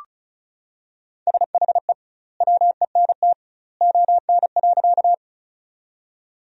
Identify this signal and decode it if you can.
Morse code, heard as a keyed tone, about 36 wpm, 700 Hz. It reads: SHE WENT ON.
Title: 2x Word-Spacing - Sentences from Top 200 Words - 35wpm